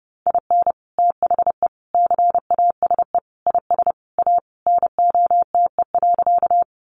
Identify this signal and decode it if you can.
Morse code, heard as a keyed tone, about 30 wpm, 700 Hz. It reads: IN THE CASE IS A NOTE.
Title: Sentences from Top 700 Words 30wpm